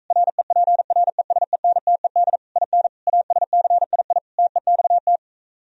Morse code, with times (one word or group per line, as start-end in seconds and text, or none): 0.1-2.4 REPRESENTED
2.6-2.9 IN
3.1-4.2 ASCII
4.4-5.2 TEXT